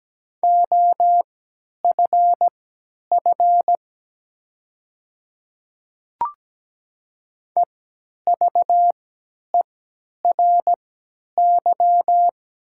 Morse code, heard as a keyed tone, about 17 wpm, spaced as for 10 wpm, 700 Hz. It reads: OFF EVERY
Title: 3x Character Spacing - Top 200 Words - 17wpm